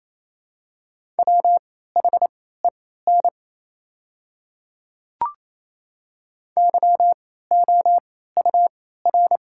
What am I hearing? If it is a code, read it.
Morse code, heard as a keyed tone, about 27 wpm, spaced as for 16 wpm, 690 Hz. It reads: WHEN YOUR
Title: 3x Character Spacing - Top 100 Words - 28wpm